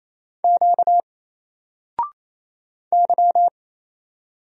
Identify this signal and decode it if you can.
Morse code, heard as a keyed tone, about 28 wpm, 700 Hz. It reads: Q Y
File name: Single-Letter-Number-28wpm